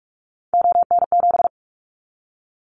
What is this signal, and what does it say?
Morse code, heard as a keyed tone, about 46 wpm, 700 Hz. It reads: ON7
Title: Callsign Prefixes for Top 100 DXCC - 45wpm